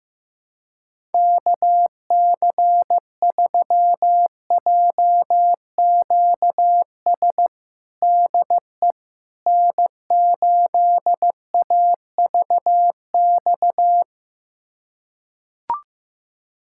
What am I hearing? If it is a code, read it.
Morse code, heard as a keyed tone, about 15 wpm, 700 Hz. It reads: KC3JQS DE N8AVX
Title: USA Callsign DE Callsign 15wpm